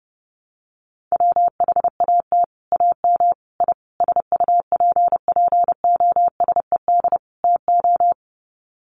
Morse code, heard as a keyed tone, about 30 wpm, 700 Hz. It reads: WHAT AM I SUPPOSED TO
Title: Sets of 5 Words Encore2 30wpm